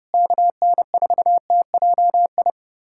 Morse code, heard as a keyed tone, about 32 wpm, 700 Hz. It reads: KN4TJI
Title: USA Callsign DE Callsign 30wpm